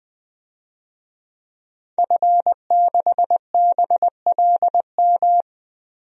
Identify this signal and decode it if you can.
Morse code, heard as a keyed tone, about 20 wpm, 700 Hz. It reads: F6BLM
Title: DXCC-100 Call Signs - 5 Characters - 20wpm